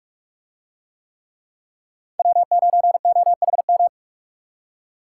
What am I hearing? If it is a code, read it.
Morse code, heard as a keyed tone, about 46 wpm, 700 Hz. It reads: W9OHM